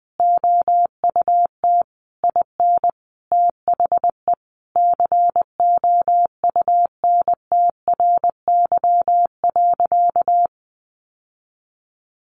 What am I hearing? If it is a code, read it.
Morse code, heard as a keyed tone, about 20 wpm, 700 Hz. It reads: OUT IN THE COUNTRY.
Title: Sentences from Top 700 Words 20wpm